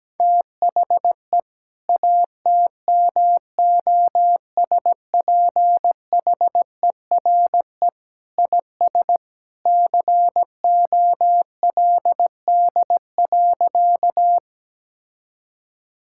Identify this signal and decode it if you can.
Morse code, heard as a keyed tone, about 17 wpm, 700 Hz. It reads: THE ATMOSPHERE IS COLD.
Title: Sentences from Top 1600 Words 17wpm